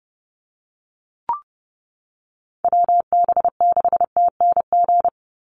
Morse code, heard as a keyed tone, about 30 wpm, 700 Hz. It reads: WB6TNG
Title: USA Call Signs 30wpm